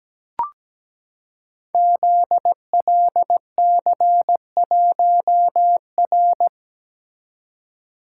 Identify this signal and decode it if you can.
Morse code, heard as a keyed tone, about 17 wpm, 700 Hz. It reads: ZLC1R
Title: Code Groups - 5 Letter:Numbers - 17wpm